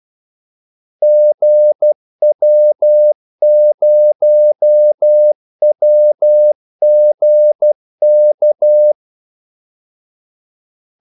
Morse code, heard as a keyed tone, about 12 wpm, 600 Hz. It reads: GW0WGK
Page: 2 ekstra ordmellemrum mellem kaldesignalerne